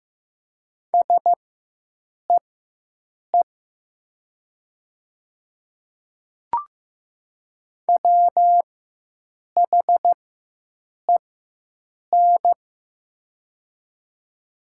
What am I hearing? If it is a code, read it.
Morse code, heard as a keyed tone, about 15 wpm, spaced as for 7 wpm, 700 Hz. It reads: SEE WHEN
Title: 4x Character Spacing - Top 200 Words - 15wpm